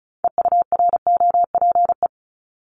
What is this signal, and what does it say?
Morse code, heard as a keyed tone, about 36 wpm, 700 Hz. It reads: EUROPE
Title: Multi-letter Phonemes 35wpm